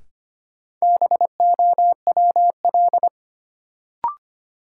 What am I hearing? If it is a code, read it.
Morse code, heard as a keyed tone, about 25 wpm, 700 Hz. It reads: BOWL
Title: Sentences from Top 1200 Words - New Word Drill - 25wpm